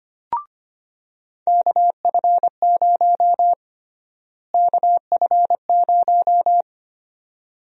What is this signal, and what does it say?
Morse code, heard as a keyed tone, about 25 wpm, 700 Hz. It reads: KF0 KF0
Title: USA Call Sign Prefixes - Repeated 2x - 25wpm